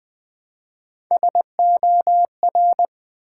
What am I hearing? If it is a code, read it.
Morse code, heard as a keyed tone, about 20 wpm, 700 Hz. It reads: SOR
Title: Build A Word - 1 Letter At A Time - 4-Letter Words - 20wpm